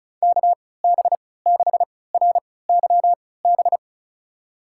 Morse code, heard as a keyed tone, about 36 wpm, spaced as for 20 wpm, 700 Hz. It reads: KB6RYB
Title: USA Call Signs - 6 Characters - 3x Character Spacing - 35wpm